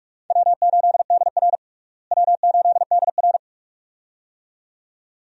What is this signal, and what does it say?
Morse code, heard as a keyed tone, about 43 wpm, 700 Hz. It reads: W8DR W8DR
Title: USA Call Signs - 4 Characters - Repeated 2x - 45wpm